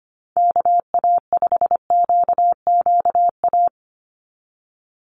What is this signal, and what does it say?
Morse code, heard as a keyed tone, about 25 wpm, 700 Hz. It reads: KA5QQA
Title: NA SOTA Chaser Callsigns 25wpm